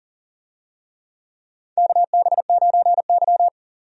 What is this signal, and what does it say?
Morse code, heard as a keyed tone, about 40 wpm, 700 Hz. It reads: KB9Y
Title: USA Call Signs - 4 Characters - 40wpm